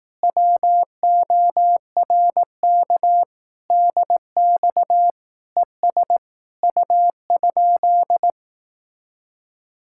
Morse code, heard as a keyed tone, about 18 wpm, 700 Hz. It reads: WORK DX ES U?
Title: Speed-Racing Common QSO Phrases Encore - 15wpm